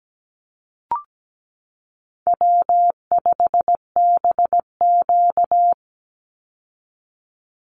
Morse code, heard as a keyed tone, about 17 wpm, 700 Hz. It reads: W5BQ
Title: Easy Call Signs Repeated 3x - 17wpm